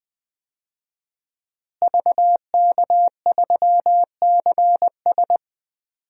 Morse code, heard as a keyed tone, about 19 wpm, 700 Hz. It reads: VK3CS
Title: DXCC-100 Call Signs - 5 Characters - Repeated 2x - 20wpm